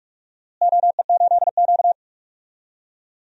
Morse code, heard as a keyed tone, about 46 wpm, 700 Hz. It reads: OE8Q
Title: DXCC-100 Call Signs - 4 Characters - Repeated 3x - 45wpm